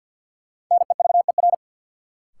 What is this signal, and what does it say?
Morse code, heard as a keyed tone, about 53 wpm, 700 Hz. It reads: NEVER
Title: Sentences from Top 300 Words - New Word Drill - 50wpm